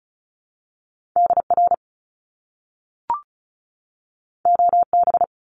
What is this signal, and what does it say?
Morse code, heard as a keyed tone, about 36 wpm, 700 Hz. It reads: DR OB
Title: Top Two Letter Combinations in Words 35wpm